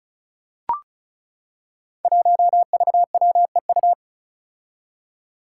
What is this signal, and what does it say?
Morse code, heard as a keyed tone, about 36 wpm, 700 Hz. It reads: 1VWEU
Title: Five Challenging Letter-Number Combinations 35wpm